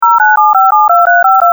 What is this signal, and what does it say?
DTMF, heard as telephone touch tones, tones of about 180 ms. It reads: 097572352
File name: voice / shenghuo